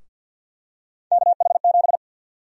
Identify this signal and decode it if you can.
Morse code, heard as a keyed tone, about 52 wpm, 700 Hz. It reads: KH7
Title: USA Call Sign Prefixes - 50wpm